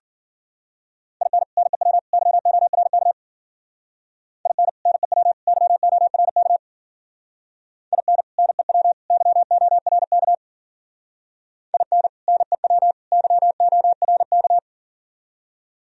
Morse code, heard as a keyed tone, about 51 wpm, 700 Hz. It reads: IN NEW YORK IN NEW YORK IN NEW YORK IN NEW YORK